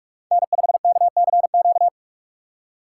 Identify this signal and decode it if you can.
Morse code, heard as a keyed tone, about 44 wpm, 700 Hz. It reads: N5KCQ
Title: USA Call Signs - 6x Character Spacing - 45wpm